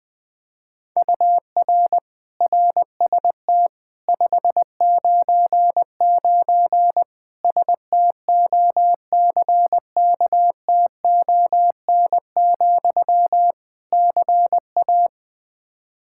Morse code, heard as a keyed tone, about 20 wpm, 700 Hz. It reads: UR RST 599 STOCKTON, CA
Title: K1USN SST QSO Practice 20wpm